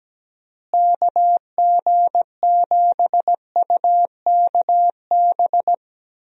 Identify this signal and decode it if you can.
Morse code, heard as a keyed tone, about 17 wpm, 700 Hz. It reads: KG7UKB
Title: USA Call Signs - 6 Characters - Repeated 3x - 17wpm